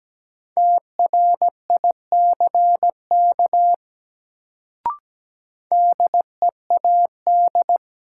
Morse code, heard as a keyed tone, about 17 wpm, 700 Hz. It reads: TRICK DEAD
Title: Sentences from Top 1500 Words - New Word Drill - 17wpm